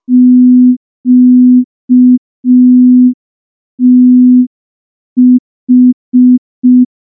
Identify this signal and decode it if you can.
Morse code, heard as a keyed tone, about 5 wpm, 250 Hz. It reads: QTH
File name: morse / morse-training-05WPM